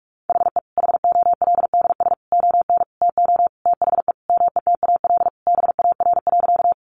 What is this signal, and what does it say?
Morse code, heard as a keyed tone, about 44 wpm, 700 Hz. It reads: HE HOLDS ON TO THE METAL BAR.
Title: Additional Easy Phrases 45wpm